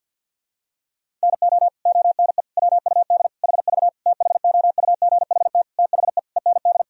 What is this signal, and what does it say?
Morse code, heard as a keyed tone, about 52 wpm, 700 Hz. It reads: NO ONE WUD HV THOUGHT THE END
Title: Sentences from Top 300 Words and Abbreviations 50wpm